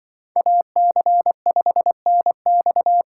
Morse code, heard as a keyed tone, about 24 wpm, 700 Hz. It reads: AC5NX
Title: Speed Racing - USA 2x2 Call Signs 20wpm